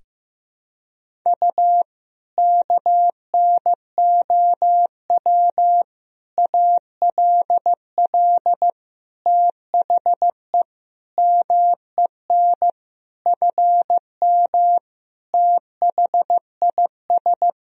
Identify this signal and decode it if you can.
Morse code, heard as a keyed tone, about 15 wpm, 700 Hz. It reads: U KNOW ALL THE MEN FM THIS
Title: Sentences from Top 300 Words and Abbreviations 15wpm